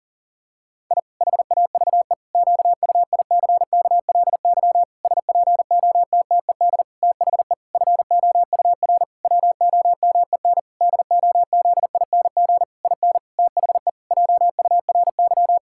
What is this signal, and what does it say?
Morse code, heard as a keyed tone, about 37 wpm, 700 Hz. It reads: I HAVE QUICKLY SPOTTED THE FOUR WOMEN DOZING IN THE JURY
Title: Pangram Challenge v2 - 40wpm